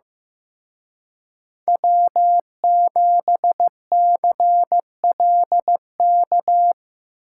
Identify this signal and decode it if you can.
Morse code, heard as a keyed tone, about 17 wpm, 700 Hz. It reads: W7CLK
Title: Speed Racing - USA 1x3 Call Signs 15wpm